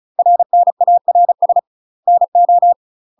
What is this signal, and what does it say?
Morse code, heard as a keyed tone, about 35 wpm, 700 Hz. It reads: RNARS NO
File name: UK MASTER QSOs TRAINING / 35WPM / UK MASTER QSOs 0018